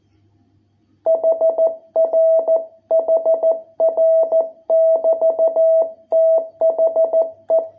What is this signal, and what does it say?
Morse code, heard as a keyed tone, about 14 wpm, 660 Hz. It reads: HR HR <BT> THE